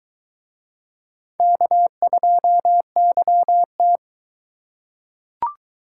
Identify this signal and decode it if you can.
Morse code, heard as a keyed tone, about 22 wpm, 700 Hz. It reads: K2YT